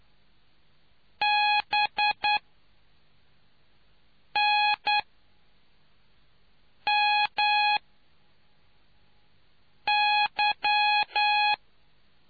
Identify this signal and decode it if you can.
Morse code, heard as a keyed tone, about 9 wpm, 840 Hz. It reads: B N M Y